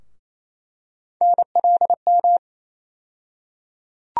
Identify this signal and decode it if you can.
Morse code, heard as a keyed tone, about 28 wpm, 700 Hz. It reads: NLM
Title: USA Call Sign Postfixes - 3 Letters 28wpm